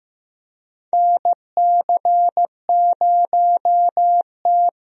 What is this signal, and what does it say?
Morse code, heard as a keyed tone, about 15 wpm, 700 Hz. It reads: NC0T
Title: USA Call Signs - 4 Characters - Repeated 3x - 15wpm